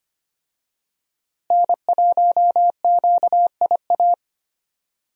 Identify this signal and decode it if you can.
Morse code, heard as a keyed tone, about 25 wpm, 700 Hz. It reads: N1QIA
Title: USA Call Signs - 5 Characters - Repeated 3x - 25wpm